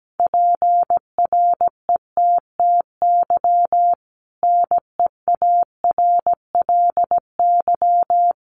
Morse code, heard as a keyed tone, about 17 wpm, 700 Hz. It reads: PRETTY NEARLY
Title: Sentences from Top 1300 Words 17wpm